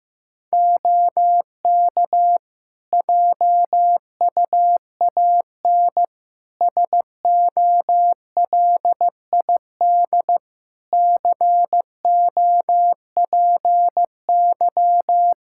Morse code, heard as a keyed tone, about 16 wpm, 700 Hz. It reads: OK JUAN SOLID COPY
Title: K1USN SST QSO Practice 15wpm